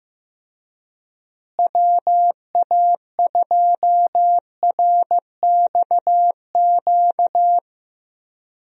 Morse code, heard as a keyed tone, about 15 wpm, 700 Hz. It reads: WA2RXQ
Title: USA Call Signs - 6 Characters - Repeated 2x - 15wpm